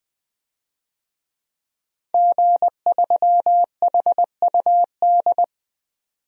Morse code, heard as a keyed tone, about 20 wpm, 700 Hz. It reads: G3HUD
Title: DXCC-100 Call Signs - 5 Characters - 20wpm